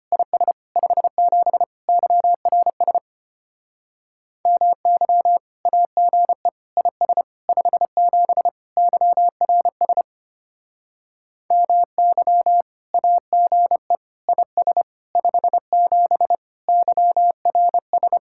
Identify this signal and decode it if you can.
Morse code, heard as a keyed tone, about 32 wpm, 700 Hz. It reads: IS 57 YRS MY AGE IS 57 YRS MY AGE IS 57 YRS